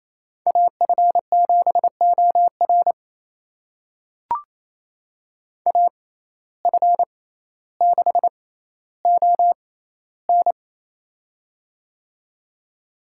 Morse code, heard as a keyed tone, about 28 wpm, spaced as for 14 wpm, 700 Hz. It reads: AF7OR AF6ON
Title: USA Call Signs - 5 Characters - 6x Character Spacing - 28wpm